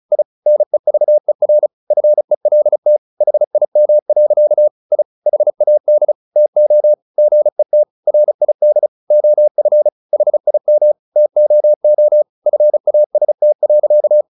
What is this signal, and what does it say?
Morse code, heard as a keyed tone, about 34 wpm, 600 Hz. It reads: I NEVER FELT HIM. I HAD TO GET RID OF HIM TOO FAST.
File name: the old man and the sea / the old man and the sea 1 35wpm